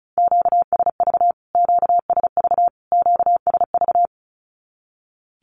Morse code, heard as a keyed tone, about 34 wpm, 700 Hz. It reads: QSV QSV QSV
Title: Q-Codes Repeated 3x 35wpm